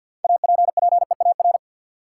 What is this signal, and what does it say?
Morse code, heard as a keyed tone, about 53 wpm, 700 Hz. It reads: APPEAR
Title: Sentences from Top 1000 Words - Review Part 2 - New Word Drill - 50wpm